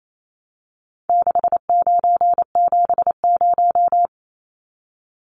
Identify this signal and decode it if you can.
Morse code, heard as a keyed tone, about 28 wpm, 700 Hz. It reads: 6970
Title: Numbers - 4-Digit 28wpm